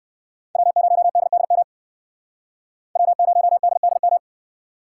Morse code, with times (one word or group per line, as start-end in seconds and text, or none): 0.6-1.6 W0DDM
3.0-4.2 W0DDM